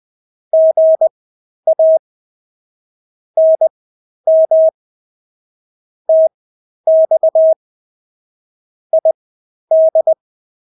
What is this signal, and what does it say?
Morse code, heard as a keyed tone, about 20 wpm, spaced as for 11 wpm, 640 Hz. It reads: GA NM TX ID
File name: State-Abbreviations-10